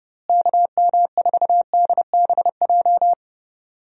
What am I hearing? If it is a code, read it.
Morse code, heard as a keyed tone, about 30 wpm, 700 Hz. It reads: KM4DBJ
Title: USA Call Signs - 6 Characters - Repeated 2x - 30wpm